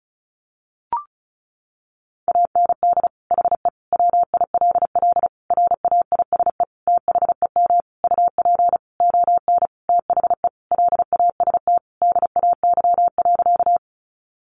Morse code, heard as a keyed tone, about 34 wpm, 700 Hz. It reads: AND HE WILL RAISE THEM UP ON THE LAST DAY.
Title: Sentences from Top 1400 Words 35wpm